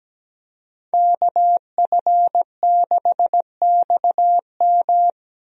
Morse code, heard as a keyed tone, about 17 wpm, 700 Hz. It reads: KF6XM
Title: USA Call Signs - 5 Characters - Repeated 3x - 17wpm